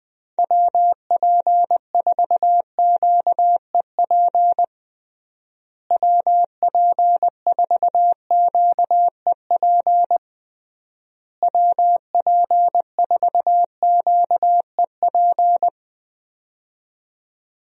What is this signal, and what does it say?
Morse code, heard as a keyed tone, about 21 wpm, 700 Hz. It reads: WP4QEP WP4QEP WP4QEP
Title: USA Call Signs - 6 Characters - Repeated 3x - 20wpm